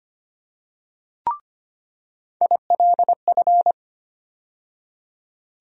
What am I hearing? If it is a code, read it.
Morse code, heard as a keyed tone, about 25 wpm, 700 Hz. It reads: ILF
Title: USA Call Sign Postfixes - 3 Letters 25wpm